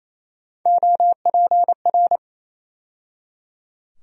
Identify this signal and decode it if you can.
Morse code, heard as a keyed tone, about 28 wpm, 700 Hz. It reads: OPR
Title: USA Call Sign Postfixes - 3 Letters 28wpm